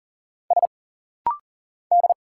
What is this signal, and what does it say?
Morse code, heard as a keyed tone, about 40 wpm, 700 Hz. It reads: S D
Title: Single Letters - Rapid-Fire 40wpm